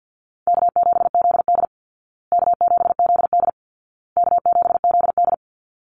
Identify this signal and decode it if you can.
Morse code, heard as a keyed tone, about 50 wpm, 700 Hz. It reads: K7ZD K7ZD K7ZD
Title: Easy Call Signs Repeated 3x - 50wpm